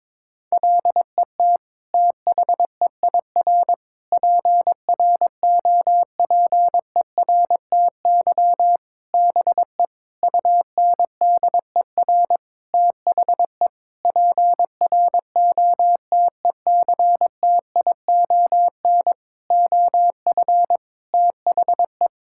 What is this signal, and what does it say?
Morse code, heard as a keyed tone, about 22 wpm, 700 Hz. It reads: LET THEIR PROPERTY BE UNDER THE PROTECTION OF THE
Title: Sentences from Top 1600 Words 22wpm